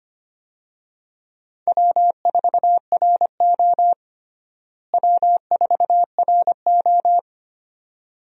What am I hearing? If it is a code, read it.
Morse code, heard as a keyed tone, about 25 wpm, 700 Hz. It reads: W4RO W4RO
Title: USA Call Signs - 4 Characters - Repeated 2x - 25wpm